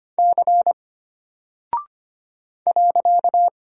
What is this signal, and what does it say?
Morse code, heard as a keyed tone, about 25 wpm, 700 Hz. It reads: C .
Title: Single Character (Letter:Number:Punctuation:Prosign) Alt 25wpm